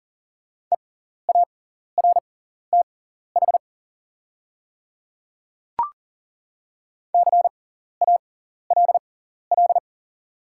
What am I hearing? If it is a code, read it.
Morse code, heard as a keyed tone, about 40 wpm, spaced as for 14 wpm, 690 Hz. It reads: EARTH CALL
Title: 6x Character Spacing - Top 200 Words - 40wpm